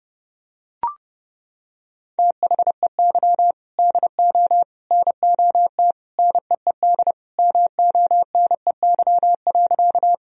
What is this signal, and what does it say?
Morse code, heard as a keyed tone, about 30 wpm, 700 Hz. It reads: THEY DO NOT NEED MONEY.
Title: Sentences from Top 1000 Words - Review Part 1 - 30wpm